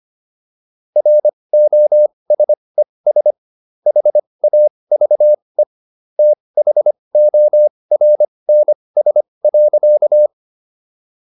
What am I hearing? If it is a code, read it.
Morse code, heard as a keyed tone, about 25 wpm, spaced as for 20 wpm, 600 Hz. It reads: ROSES HAVE THORNS.